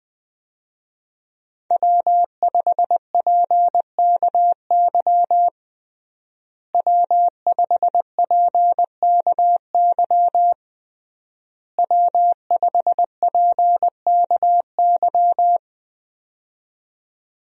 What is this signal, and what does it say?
Morse code, heard as a keyed tone, about 21 wpm, 700 Hz. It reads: W5PKY W5PKY W5PKY
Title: USA Call Signs - 5 Characters - Repeated 3x - 20wpm